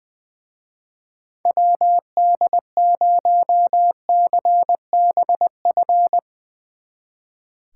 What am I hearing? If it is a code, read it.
Morse code, heard as a keyed tone, about 22 wpm, 700 Hz. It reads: WD0CBF
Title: Speed Racing - USA Call Signs 20wpm